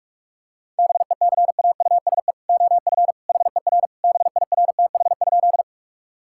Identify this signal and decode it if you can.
Morse code, heard as a keyed tone, about 46 wpm, 700 Hz. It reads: BECAUSE OF HER BIRTH?
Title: Sentences from Top 1000 Words - Review Part 5 - 45wpm